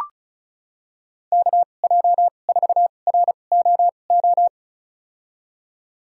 Morse code, heard as a keyed tone, about 36 wpm, spaced as for 25 wpm, 700 Hz. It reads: KJ4ROO
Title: USA Call Signs - 6 Characters - 2x Character Spacing - 35wpm